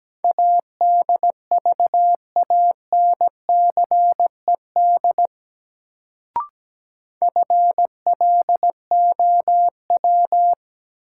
Morse code, heard as a keyed tone, about 17 wpm, 700 Hz. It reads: ADVANCED FLOW
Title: Sentences from Top 1200 Words - New Word Drill Alt - 17wpm